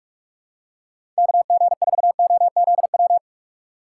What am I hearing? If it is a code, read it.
Morse code, heard as a keyed tone, about 47 wpm, 700 Hz. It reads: KG4OZW
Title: USA Call Signs - 6 Characters - 4x Character Spacing - 45wpm